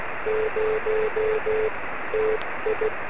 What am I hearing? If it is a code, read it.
Morse code, heard as a keyed tone, about 16 wpm, 430 Hz, with noise. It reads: 0 TI
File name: sm7lcb / 08-db6nt-23cm